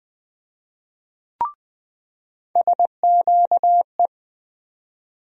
Morse code, heard as a keyed tone, about 20 wpm, 700 Hz. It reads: SQE
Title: USA Call Sign Postfixes - 3 Letters 20wpm